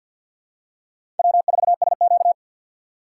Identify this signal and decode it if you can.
Morse code, heard as a keyed tone, about 48 wpm, 700 Hz. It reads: W4SQ
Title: USA Call Signs - 6x Character Spacing - 50wpm